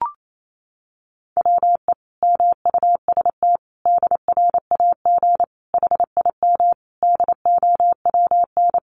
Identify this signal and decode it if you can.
Morse code, heard as a keyed tone, about 28 wpm, 700 Hz. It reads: WE MUST DRAG HIM DOWN
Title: Sentences from Top 1500 Words 28wpm